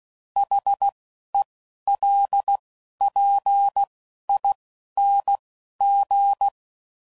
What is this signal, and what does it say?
Morse code, heard as a keyed tone, about 16 wpm, 800 Hz. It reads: H E L P I N G
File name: Suffixes / Special INGs / Single INGs